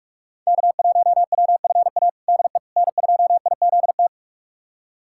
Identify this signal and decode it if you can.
Morse code, heard as a keyed tone, about 41 wpm, 700 Hz. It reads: K1WFA DE N2IZT
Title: USA Callsign DE Callsign - 5 Characters - 45wpm